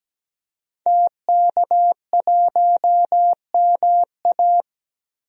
Morse code, heard as a keyed tone, about 17 wpm, 700 Hz. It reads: TK1MA
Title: DXCC-100 Call Signs - 5 Characters - 17wpm